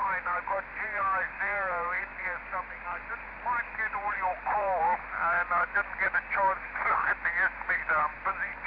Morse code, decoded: none